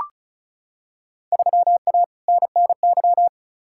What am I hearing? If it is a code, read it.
Morse code, heard as a keyed tone, about 36 wpm, 700 Hz. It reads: 3A NNY
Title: Field Day Exchanges - 35wpm